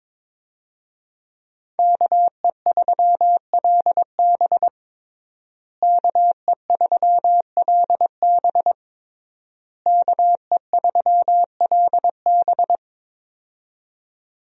Morse code, heard as a keyed tone, about 22 wpm, 700 Hz. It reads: KE3LB KE3LB KE3LB
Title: USA Call Signs - 5 Characters - Repeated 3x - 22wpm